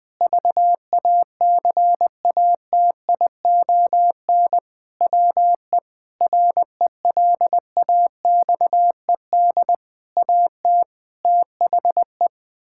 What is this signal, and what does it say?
Morse code, heard as a keyed tone, about 21 wpm, 700 Hz. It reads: VACATION WE RELAXED AT THE
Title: Sentences with 6-Word Phrases - 20wpm